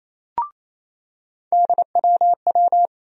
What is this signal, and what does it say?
Morse code, heard as a keyed tone, about 28 wpm, 700 Hz. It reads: DWW
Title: USA Call Sign Postfixes - 3 Letters 28wpm